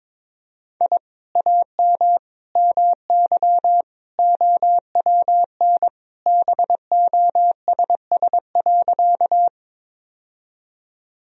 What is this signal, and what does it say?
Morse code, heard as a keyed tone, about 22 wpm, 700 Hz. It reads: I AM MY OWN BOSS.
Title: Sentences from Top 1000 Words - Review Part 4 - 22wpm